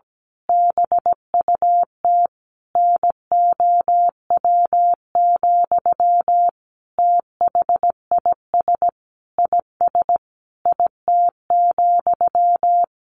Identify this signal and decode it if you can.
Morse code, heard as a keyed tone, about 17 wpm, 700 Hz. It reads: BUT NOW, THIS IS IT,
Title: A Sound Of Thunder 17wpm